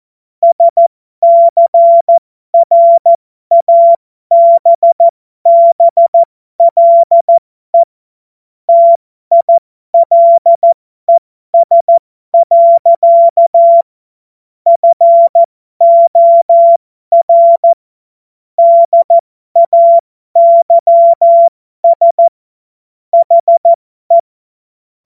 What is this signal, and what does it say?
Morse code, heard as a keyed tone, about 14 wpm, spaced as for 12 wpm, 680 Hz. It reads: SCRABBLE TILES. FOR DAYS HE
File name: Morsekurs / Witze